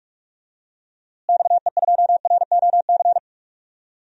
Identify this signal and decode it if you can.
Morse code, heard as a keyed tone, about 46 wpm, 700 Hz. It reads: XE2ROC
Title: DXCC-100 Call Signs - 6 Characters - Repeated 2x - 45wpm